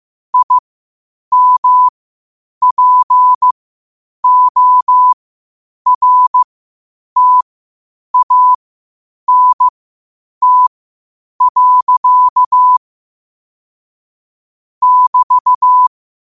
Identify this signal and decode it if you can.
Morse code, heard as a keyed tone, about 15 wpm, spaced as for 8 wpm, 1000 Hz. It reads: IMPORTANT. <BT>
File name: qotdfiles / 8wpm